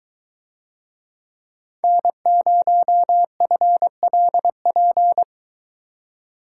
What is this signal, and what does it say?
Morse code, heard as a keyed tone, about 23 wpm, 700 Hz. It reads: N0FLP